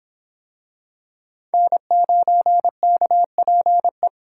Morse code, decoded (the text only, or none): N9KPE